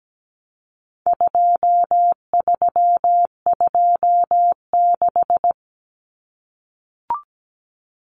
Morse code, heard as a keyed tone, about 17 wpm, 700 Hz. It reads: 2326
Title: Common Cut Numbers - 4-Digit 17wpm